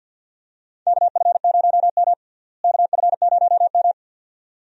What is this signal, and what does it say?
Morse code, heard as a keyed tone, about 52 wpm, 700 Hz. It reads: KF0M KF0M
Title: USA Call Signs - 4 Characters - Repeated 2x - 50wpm